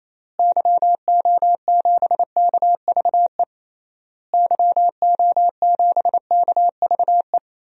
Morse code, heard as a keyed tone, about 28 wpm, 700 Hz. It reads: YO7KVE YO7KVE
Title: DXCC-100 Call Signs - 6 Characters - Repeated 3x - 28wpm